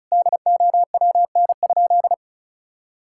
Morse code, heard as a keyed tone, about 36 wpm, 700 Hz. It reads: DOWN?
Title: Sentences from Top 1000 Words - Review Part 6 - 35wpm